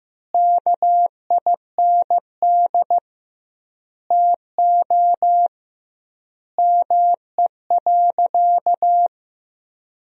Morse code, heard as a keyed tone, about 15 wpm, 700 Hz. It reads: KIND TO ME.
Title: 2x Word-Spacing - Sentences from Top 200 Words - 15wpm